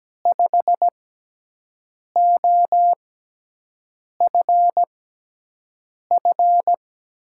Morse code, heard as a keyed tone, about 17 wpm, spaced as for 6 wpm, 700 Hz. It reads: 5OFF
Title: USA Call Signs - 6x Character Spacing - 17wpm